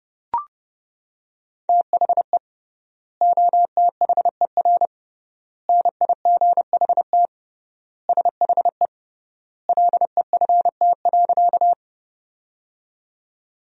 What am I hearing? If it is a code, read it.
Morse code, heard as a keyed tone, about 28 wpm, 700 Hz. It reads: THE OTHER NIGHT SHE LEFT.
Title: 3x Word-Spacing - Sentences from Top 300 Words - 30wpm